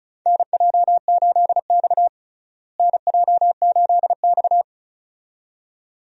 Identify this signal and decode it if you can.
Morse code, heard as a keyed tone, about 36 wpm, 700 Hz. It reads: NJ8X NJ8X
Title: USA Call Signs - 4 Characters - Repeated 2x - 35wpm